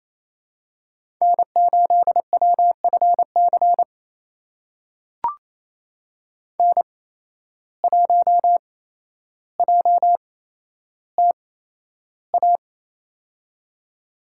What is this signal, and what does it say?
Morse code, heard as a keyed tone, about 28 wpm, spaced as for 12 wpm, 700 Hz. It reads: N8WFC N1JTA
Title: USA Call Signs - 5 Characters - 8x Character Spacing - 28wpm